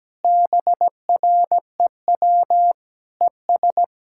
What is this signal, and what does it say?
Morse code, heard as a keyed tone, about 17 wpm, 700 Hz. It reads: BREW ES
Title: Practice QSO - Exchange 2 - Info - 17wpm